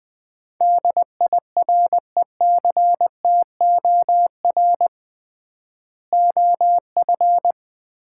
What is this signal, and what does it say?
Morse code, heard as a keyed tone, about 20 wpm, 700 Hz. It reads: DIRECTOR OF
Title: 3x Word-Spacing - Sets of 3 Words - 20wpm